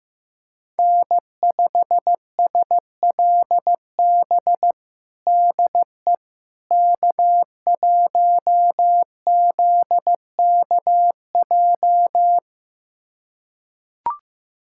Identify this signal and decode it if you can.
Morse code, heard as a keyed tone, about 15 wpm, 700 Hz. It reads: N5SLB DE K1ZKJ
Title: USA Callsign DE Callsign 15wpm